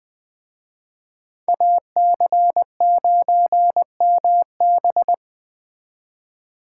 Morse code, heard as a keyed tone, about 20 wpm, 700 Hz. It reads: AC9MB